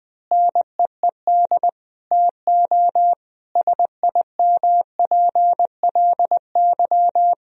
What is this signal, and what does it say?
Morse code, heard as a keyed tone, about 20 wpm, 700 Hz. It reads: NEED TO SIMPLY
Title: Sentences from Top 1000 Words - Review Part 4 - 20wpm